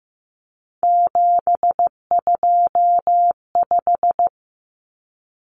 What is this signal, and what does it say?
Morse code, heard as a keyed tone, about 15 wpm, 700 Hz. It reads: 725
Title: Common Cut Numbers - 3-Digit 15wpm